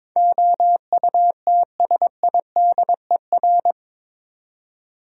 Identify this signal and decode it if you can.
Morse code, heard as a keyed tone, about 21 wpm, 700 Hz. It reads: OUTSIDER
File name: practice / 22wpm / Eight Letter Words - 22wpm